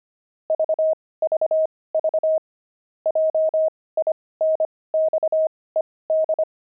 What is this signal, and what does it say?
Morse code, heard as a keyed tone, about 25 wpm, spaced as for 18 wpm, 630 Hz. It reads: VVV JINXED
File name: pangram-fs-18-cs-25-jinxed